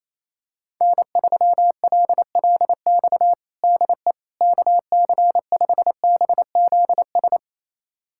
Morse code, heard as a keyed tone, about 27 wpm, 700 Hz. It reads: N3LLX DE KC5BZS
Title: USA Callsign DE Callsign 28wpm